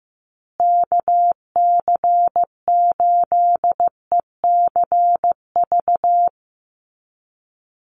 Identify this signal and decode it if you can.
Morse code, heard as a keyed tone, about 15 wpm, 700 Hz. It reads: KC8ECV
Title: USA Call Signs Repeated 2x 15wpm